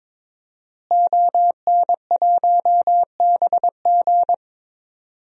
Morse code, heard as a keyed tone, about 23 wpm, 700 Hz. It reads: ON1BG
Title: DXCC-100 Call Signs - 5 Characters - Repeated 3x - 22wpm